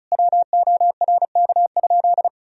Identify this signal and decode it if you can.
Morse code, heard as a keyed tone, about 34 wpm, 700 Hz. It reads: WORK?
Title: Sentences from Top 1000 Words - Review Part 4 - 35wpm